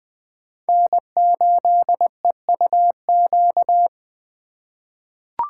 Morse code, heard as a keyed tone, about 20 wpm, 700 Hz. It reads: N8EUQ
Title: USA Call Signs - 5 Characters - 3x Character Spacing - 20wpm